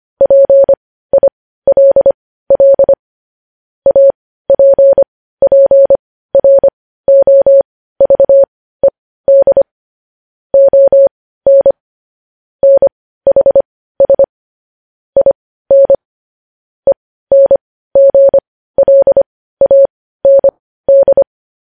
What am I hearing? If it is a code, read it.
Morse code, heard as a keyed tone, about 25 wpm, spaced as for 15 wpm, 550 Hz. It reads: PILL APPROVED ON NHS IN ENGLAND
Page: news headlines in morse code at 15 wpm